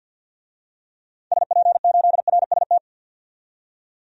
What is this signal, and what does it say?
Morse code, heard as a keyed tone, about 50 wpm, 700 Hz. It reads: SP8RST
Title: DXCC-100 Call Signs - 6 Characters - 50wpm